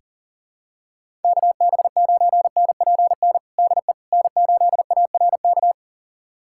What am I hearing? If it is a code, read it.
Morse code, heard as a keyed tone, about 40 wpm, 700 Hz. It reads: KB9NPN DE N8ARK
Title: USA Callsign DE Callsign 40wpm